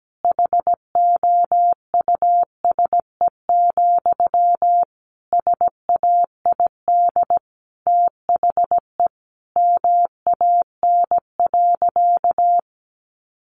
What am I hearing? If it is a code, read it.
Morse code, heard as a keyed tone, about 17 wpm, 700 Hz. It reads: HOUSE, SAID THE MAN.
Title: Sentences from Top 400 Words 17wpm